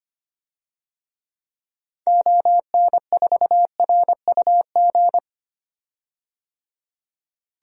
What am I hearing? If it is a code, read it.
Morse code, heard as a keyed tone, about 25 wpm, 700 Hz. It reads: ON4RUG
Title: DXCC-100 Call Signs - 6 Characters - 25wpm